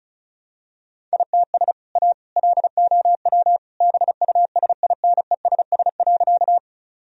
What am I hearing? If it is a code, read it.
Morse code, heard as a keyed tone, about 34 wpm, 700 Hz. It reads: ITS A LOW BUSINESS.
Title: Sentences from Top 1000 Words - Review Part 1 - 35wpm